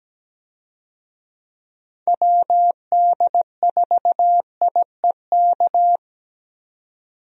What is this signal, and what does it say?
Morse code, heard as a keyed tone, about 17 wpm, 700 Hz. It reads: WD4IEK
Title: USA Call Signs - 6 Characters - 17wpm